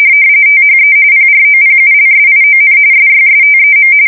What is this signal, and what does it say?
RTTY, about 45.45 baud. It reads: THE PROPULSION SYSTEM
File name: fsk170 / USS-New-Jersey